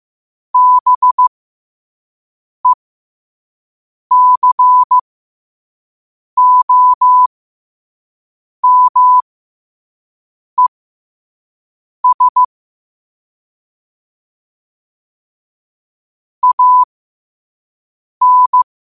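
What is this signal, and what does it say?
Morse code, heard as a keyed tone, about 15 wpm, spaced as for 5 wpm, 1000 Hz. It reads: BECOMES AN
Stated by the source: Morse code practice at 5 wpm.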